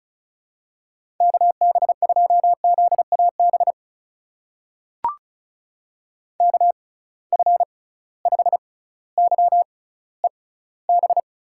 Morse code, heard as a keyed tone, about 36 wpm, spaced as for 18 wpm, 700 Hz. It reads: KB2ZAB KF5YEB
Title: USA Call Signs - 6x Character Spacing - 35wpm